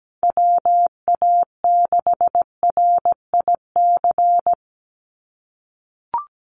Morse code, heard as a keyed tone, about 17 wpm, 700 Hz. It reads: WA6RIC
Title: NA SOTA Chaser Callsigns 17wpm